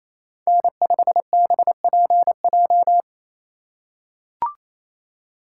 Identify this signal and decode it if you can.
Morse code, heard as a keyed tone, about 28 wpm, 700 Hz. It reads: N5BPJ
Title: USA Call Signs - 5 Characters - 4x Character Spacing - 28wpm